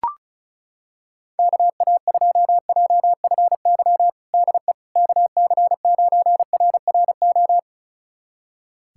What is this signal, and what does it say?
Morse code, heard as a keyed tone, about 36 wpm, 700 Hz. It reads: KA2JFY DE KC9RRO
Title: USA Callsign DE Callsign - 6 Characters - 35wpm